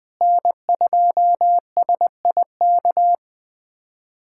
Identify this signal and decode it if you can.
Morse code, heard as a keyed tone, about 20 wpm, 700 Hz. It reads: N2SIK
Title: USA Callsign DE Callsign - 5 Characters - 20wpm